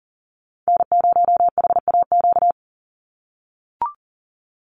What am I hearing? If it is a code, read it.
Morse code, heard as a keyed tone, about 40 wpm, 700 Hz. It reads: N0HAQ
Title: USA Call Signs 40wpm